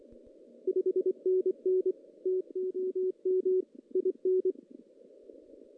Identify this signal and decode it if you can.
Morse code, heard as a keyed tone, about 25 wpm, 440 Hz. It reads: 5NN TOM IN
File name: 2016 JAN CW NAQP / 2016 Jan naqp k5ot - 0-5